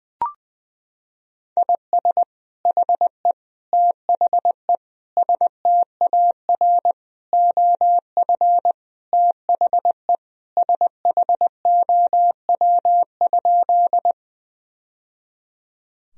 Morse code, heard as a keyed tone, about 20 wpm, 700 Hz. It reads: IS HE THE STAR OF THE SHOW?
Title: Sentences from Top 1000 Words - Review Part 2 - 20wpm